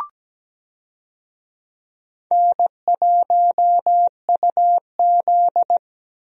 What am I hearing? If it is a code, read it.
Morse code, heard as a keyed tone, about 17 wpm, 700 Hz. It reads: N1UZ
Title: USA Call Signs - 4 Characters - Repeated 2x - 17wpm